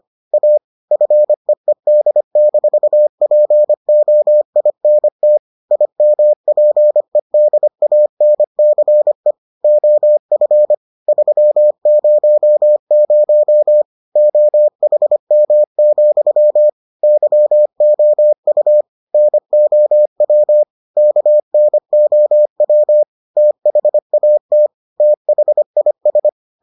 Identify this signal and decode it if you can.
Morse code, heard as a keyed tone, about 25 wpm, 600 Hz. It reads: A FEED-POINT IMPEDANCE OF 300 OHM, YOU NOW KNOW THAT THIS